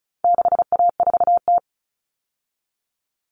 Morse code, heard as a keyed tone, about 34 wpm, 700 Hz. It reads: 6A4T
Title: Common Cut Numbers - 4-Digit 35wpm